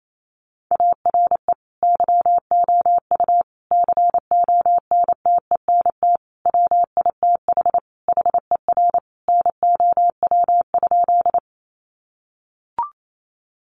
Morse code, heard as a keyed tone, about 28 wpm, 690 Hz. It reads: ARE YOU CONTENT WITH HER NOW?
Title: Sentences from Top 900 Words 28wpm